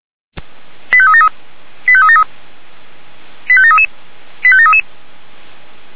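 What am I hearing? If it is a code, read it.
Five-tone selective call (ZVEI), heard as quick runs of five tones, tones of about 70 ms. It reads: 85373 85373 85733 85733
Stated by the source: ZVEI five-tone selective call received by SDR.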